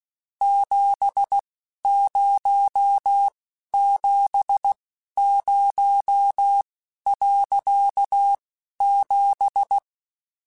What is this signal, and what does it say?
Morse code, heard as a keyed tone, about 16 wpm, spaced as for 11 wpm, 800 Hz. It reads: 7070.7